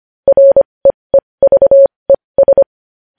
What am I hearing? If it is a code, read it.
Morse code, heard as a keyed tone, about 25 wpm, spaced as for 20 wpm, 550 Hz. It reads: REEVES
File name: podcast / News headlines-1x-25x20